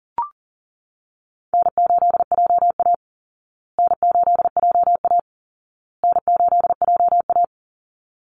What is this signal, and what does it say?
Morse code, heard as a keyed tone, about 40 wpm, 700 Hz. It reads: N8JA N8JA N8JA
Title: Easy Call Signs Repeated 3x - 40wpm